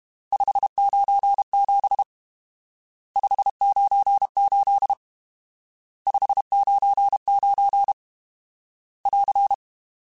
Morse code, heard as a keyed tone, about 32 wpm, 800 Hz. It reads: 597 598 599 <AR>
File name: RSTs / RST-mp3s / Various RST @30